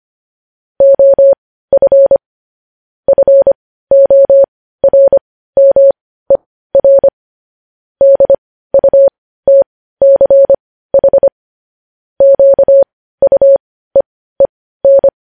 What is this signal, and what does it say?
Morse code, heard as a keyed tone, about 25 wpm, spaced as for 15 wpm, 550 Hz. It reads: OF FORMER DUTCH QUEEN
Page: News headlines at 15